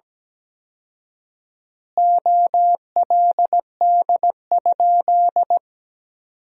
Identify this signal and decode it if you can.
Morse code, heard as a keyed tone, about 17 wpm, 700 Hz. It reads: OLD?
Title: 4x Word-Spacing - Sentences from Top 400 Words - 17wpm